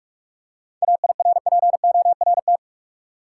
Morse code, heard as a keyed tone, about 46 wpm, 700 Hz. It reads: AIRPORT
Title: Seven Letter Words - 45wpm